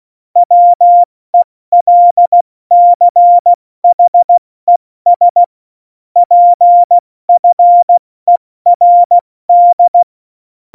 Morse code, heard as a keyed tone, about 16 wpm, 700 Hz. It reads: WELCHES PFERD
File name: Morsekurs / Witze / Witz 18 August 2020, 70 F80